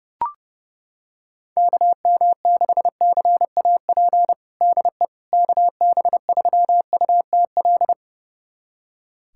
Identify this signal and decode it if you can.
Morse code, heard as a keyed tone, about 32 wpm, 700 Hz. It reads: KM6CAP DE KB3UTL
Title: USA Callsign DE Callsign 30wpm